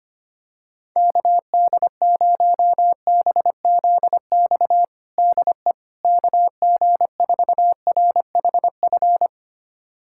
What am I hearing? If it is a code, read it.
Morse code, heard as a keyed tone, about 25 wpm, 700 Hz. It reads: KD0BZX DE KG4RHF
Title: USA Callsign DE Callsign - 6 Characters - 25wpm